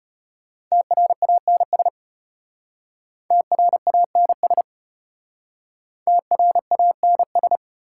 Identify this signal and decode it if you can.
Morse code, heard as a keyed tone, about 33 wpm, 700 Hz. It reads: TRANS TRANS TRANS